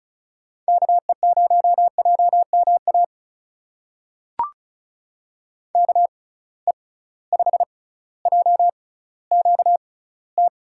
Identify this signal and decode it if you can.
Morse code, heard as a keyed tone, about 36 wpm, spaced as for 18 wpm, 700 Hz. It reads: KE0JMA KE5JQT
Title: USA Call Signs - 6 Characters - 6x Character Spacing - 35wpm